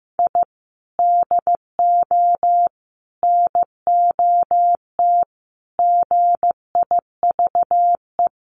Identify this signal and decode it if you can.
Morse code, heard as a keyed tone, about 15 wpm, 700 Hz. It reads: I DO NOT GIVE
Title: Sets of 5 Words Encore2 15wpm